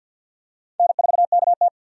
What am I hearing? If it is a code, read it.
Morse code, heard as a keyed tone, about 52 wpm, 700 Hz. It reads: N4KT
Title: USA Call Signs - 4 Characters - 50wpm